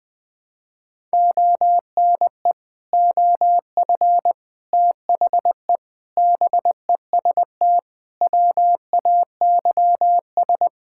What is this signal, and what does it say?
Morse code, heard as a keyed tone, about 20 wpm, 700 Hz. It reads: ONE OF THE BEST WAYS
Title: Sets of 6 Words - 20wpm